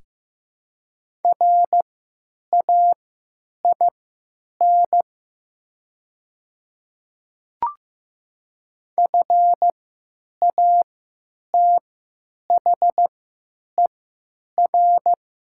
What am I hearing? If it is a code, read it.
Morse code, heard as a keyed tone, about 15 wpm, spaced as for 9 wpm, 700 Hz. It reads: RAIN FATHER
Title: 3x Character Spacing - Top 300 Words - 15wpm